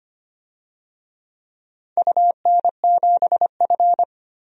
Morse code, heard as a keyed tone, about 25 wpm, 700 Hz. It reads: UN7F
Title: DXCC-100 Call Signs - 4 Characters - Repeated 2x - 25wpm